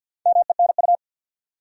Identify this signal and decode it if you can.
Morse code, heard as a keyed tone, about 53 wpm, 700 Hz. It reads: MENU
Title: Four Letter Words - 50wpm